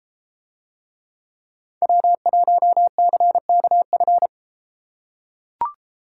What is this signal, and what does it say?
Morse code, heard as a keyed tone, about 31 wpm, 700 Hz. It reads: W1CKF